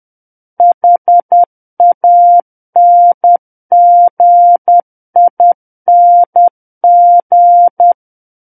Morse code, heard as a keyed tone, about 10 wpm, 700 Hz. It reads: HANGING
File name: Book One / 01-Chapter 1 - 10 WPM